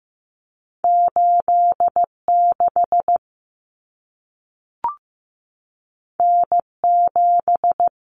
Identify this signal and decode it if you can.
Morse code, heard as a keyed tone, about 15 wpm, 700 Hz. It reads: 86 N7
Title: Common Cut Numbers - 2-Digit 15wpm